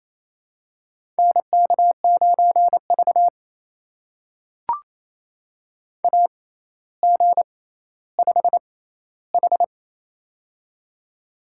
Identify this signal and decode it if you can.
Morse code, heard as a keyed tone, about 28 wpm, spaced as for 14 wpm, 700 Hz. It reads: NK9V AG5H
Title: USA Call Signs - 4 Characters - 6x Character Spacing - 28wpm